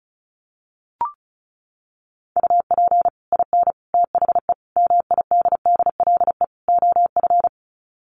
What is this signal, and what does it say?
Morse code, heard as a keyed tone, about 34 wpm, 700 Hz. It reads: UP IN THE MIDDLE OF
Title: Sets of 5 Words Encore 35wpm